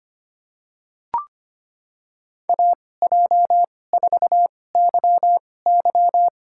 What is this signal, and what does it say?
Morse code, heard as a keyed tone, about 25 wpm, spaced as for 18 wpm, 700 Hz. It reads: AJ4YY
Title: USA Call Signs - 5 Characters - 2x Character Spacing - 25wpm